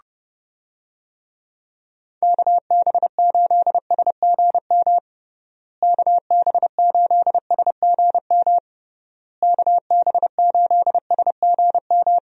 Morse code, heard as a keyed tone, about 30 wpm, 700 Hz. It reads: KB8SGM KB8SGM KB8SGM
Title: USA Call Signs - 6 Characters - Repeated 3x - 30wpm